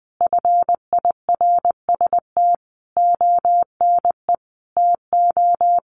Morse code, heard as a keyed tone, about 20 wpm, 700 Hz. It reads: FIRST ONE TO
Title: Sentences from Top 1300 Words 20wpm